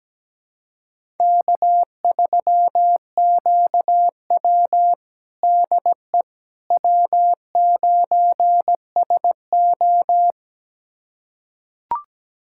Morse code, heard as a keyed tone, about 17 wpm, 700 Hz. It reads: K3QW DE W9SO
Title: USA Callsign DE Callsign - 4 Characters - 17wpm